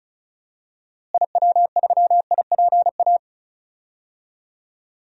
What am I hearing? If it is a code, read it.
Morse code, heard as a keyed tone, about 36 wpm, 700 Hz. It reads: IW3IPA
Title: DXCC-100 Call Signs - 6 Characters - Repeated 3x - 35wpm